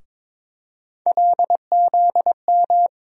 Morse code, heard as a keyed tone, about 22 wpm, 700 Hz. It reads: LZM
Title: USA Call Sign Postfixes - 3 Letters 22wpm